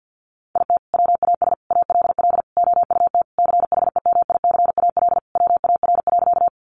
Morse code, heard as a keyed tone, about 50 wpm, 700 Hz. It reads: IT WAS ALL OUT CHEMICAL WAR.
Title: Sentences from Top 1200 Words 50wpm